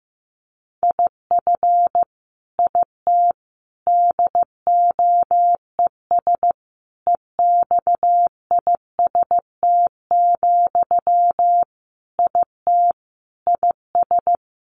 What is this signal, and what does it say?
Morse code, heard as a keyed tone, about 15 wpm, 700 Hz. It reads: IF IT DOES EXIST, IT IS
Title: Sentences from Top 1300 Words 15wpm